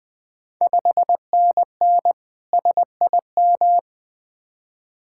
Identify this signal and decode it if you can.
Morse code, heard as a keyed tone, about 20 wpm, 700 Hz. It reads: 5NN SIM
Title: Ontario QSO Party Exchange 20wpm